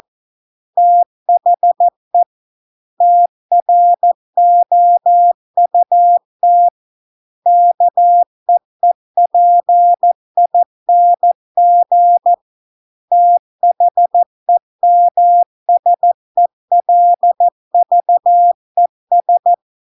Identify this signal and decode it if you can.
Morse code, heard as a keyed tone, about 14 wpm, 700 Hz. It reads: THE TROUT KEEPING THEMSELVES